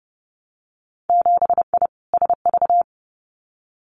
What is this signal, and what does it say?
Morse code, heard as a keyed tone, about 30 wpm, 700 Hz. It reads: 7I SV
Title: Winter Field Day Exchanges Encore 30wpm